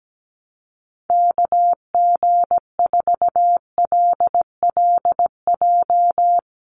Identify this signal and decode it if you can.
Morse code, heard as a keyed tone, about 17 wpm, 700 Hz. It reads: KG4LLJ
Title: USA Call Signs 17wpm